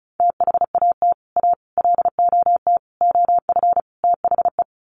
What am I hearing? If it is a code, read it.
Morse code, heard as a keyed tone, about 34 wpm, 700 Hz. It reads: THAT A LOT OF THE
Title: Sets of 5 Words Encore2 35wpm